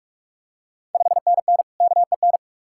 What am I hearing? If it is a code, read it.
Morse code, heard as a keyed tone, about 44 wpm, 690 Hz. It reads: 5NN KEN
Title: Ontario QSO Party Exchange 45wpm